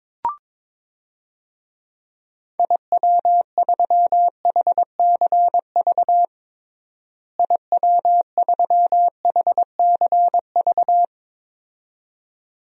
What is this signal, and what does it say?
Morse code, heard as a keyed tone, about 22 wpm, 700 Hz. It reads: IW3HCV IW3HCV
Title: DXCC-100 Call Signs - 6 Characters - Repeated 2x - 22wpm